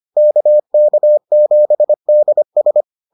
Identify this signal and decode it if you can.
Morse code, heard as a keyed tone, about 25 wpm, 600 Hz. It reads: KK7DS